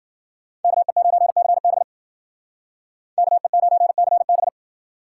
Morse code, heard as a keyed tone, about 50 wpm, 700 Hz. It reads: KE9CB KE9CB